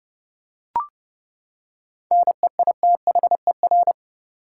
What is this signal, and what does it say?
Morse code, heard as a keyed tone, about 30 wpm, 700 Hz. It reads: NEITHER
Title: Seven Letter Words - 30wpm